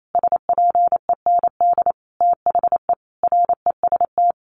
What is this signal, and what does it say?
Morse code, heard as a keyed tone, about 28 wpm, 700 Hz. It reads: SPEND THE REST
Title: Sentences from Top 1400 Words 28wpm